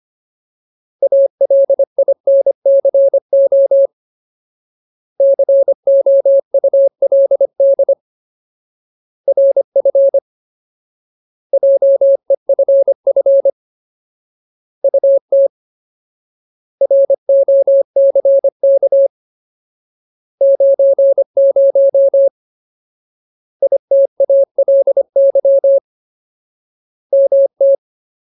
Morse code, heard as a keyed tone, about 25 wpm, 550 Hz. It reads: ALINCO COULD RF JEFF UT ROCK 90 ITALY MT
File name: Ultimate Word List / 25WPM Ultimate